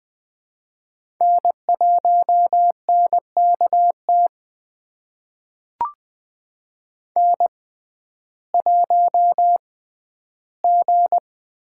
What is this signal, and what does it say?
Morse code, heard as a keyed tone, about 21 wpm, spaced as for 12 wpm, 700 Hz. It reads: N1NKT N1G
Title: USA Call Signs - 5 Characters - 6x Character Spacing - 20wpm